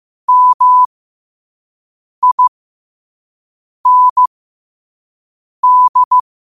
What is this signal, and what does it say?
Morse code, heard as a keyed tone, about 15 wpm, spaced as for 5 wpm, 1000 Hz. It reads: MIND
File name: qotdfiles / 5wpm